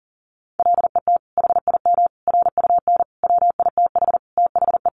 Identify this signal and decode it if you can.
Morse code, heard as a keyed tone, about 40 wpm, 700 Hz. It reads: LET HIM RUN WITH THE
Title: Sentences from Top 1300 Words 40wpm